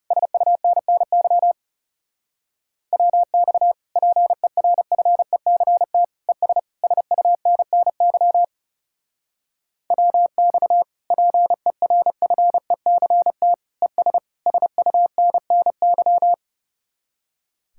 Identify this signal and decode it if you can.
Morse code, heard as a keyed tone, about 38 wpm, 700 Hz. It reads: SUNNY WX PERFECT ES SUNNY WX PERFECT ES SUNNY